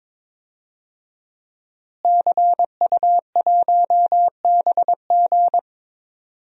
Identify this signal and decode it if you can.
Morse code, heard as a keyed tone, about 22 wpm, 700 Hz. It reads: CU1BG
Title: DXCC-100 Call Signs - 5 Characters - Repeated 2x - 22wpm